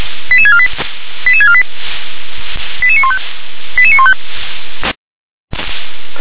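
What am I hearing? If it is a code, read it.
Five-tone selective call (ZVEI), heard as quick runs of five tones, tones of about 70 ms. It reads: 88648 88648 88915 88915
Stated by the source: ZVEI five-tone selective call received on SDR.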